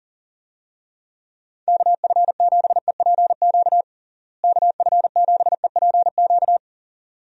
Morse code, heard as a keyed tone, about 40 wpm, 700 Hz. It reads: KF7EPQ KF7EPQ
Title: USA Call Signs - 6 Characters - Repeated 2x - 40wpm